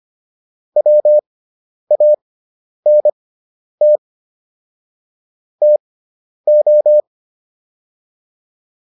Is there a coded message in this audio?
Yes: Morse code from 0.8 to 7.0 s, heard as a keyed tone, about 25 wpm, spaced as for 10 wpm, 600 Hz. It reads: WANT TO